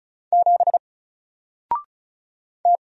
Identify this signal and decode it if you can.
Morse code, heard as a keyed tone, about 35 wpm, 700 Hz. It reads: 7 T
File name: Single-Letter-Number-35wpm